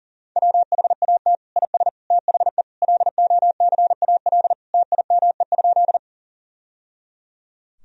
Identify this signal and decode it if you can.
Morse code, heard as a keyed tone, about 40 wpm, 700 Hz. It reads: WHAT IS THE LOCAL TIME?
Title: Sentences from Top 1000 Words - Review Part 3 - 40wpm